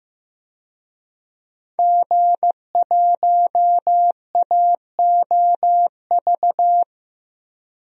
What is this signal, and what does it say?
Morse code, heard as a keyed tone, about 15 wpm, 700 Hz. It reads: G1AOV
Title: DXCC-100 Call Signs - 5 Characters - Repeated 3x - 15wpm